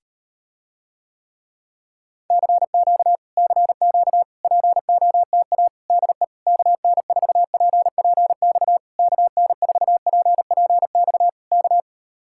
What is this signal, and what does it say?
Morse code, heard as a keyed tone, about 38 wpm, 700 Hz. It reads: CQ CQ POTA DE KN4PPX KN4PPX K